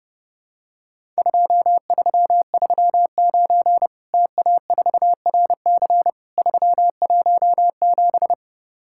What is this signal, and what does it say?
Morse code, heard as a keyed tone, about 30 wpm, 700 Hz. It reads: 2339 TA4RC 317
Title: Worked All Europe DX Contest - QTC Exchange - 30wpm